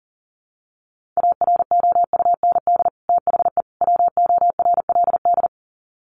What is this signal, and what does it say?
Morse code, heard as a keyed tone, about 40 wpm, 700 Hz. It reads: AROUND THE WORLD
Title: Sets of 3 Words 40wpm